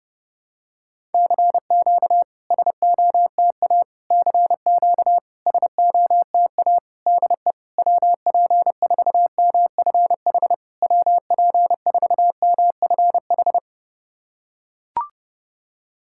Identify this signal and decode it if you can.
Morse code, heard as a keyed tone, about 30 wpm, 690 Hz. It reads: CQ SOTA CQ SOTA DE WP4MFH WP4MFH
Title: SOTA QSO Practice 30wpm